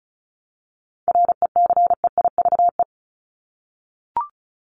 Morse code, heard as a keyed tone, about 34 wpm, 700 Hz. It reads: RECEIVE
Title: Top 751-1000 Words 35wpm